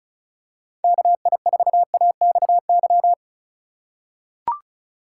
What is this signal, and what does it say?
Morse code, heard as a keyed tone, about 36 wpm, 690 Hz. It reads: KI4AXY
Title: USA Call Signs - 6 Characters - 3x Character Spacing - 35wpm